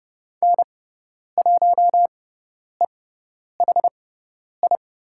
Morse code, heard as a keyed tone, about 30 wpm, spaced as for 10 wpm, 700 Hz. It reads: N1EHI
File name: Lesson 17g - Letter u - Call Signs 30wpm